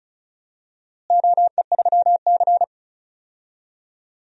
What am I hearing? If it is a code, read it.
Morse code, heard as a keyed tone, about 36 wpm, 700 Hz. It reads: OE3C
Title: DXCC-100 Call Signs - 4 Characters - Repeated 3x - 35wpm